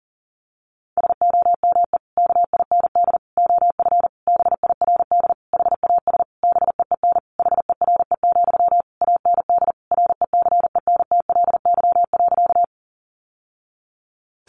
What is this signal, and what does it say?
Morse code, heard as a keyed tone, about 40 wpm, 700 Hz. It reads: SOME KIND OF BIRD HAS BEEN HERE, AND RECENTLY.
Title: Sentences from Top 1200 Words 40wpm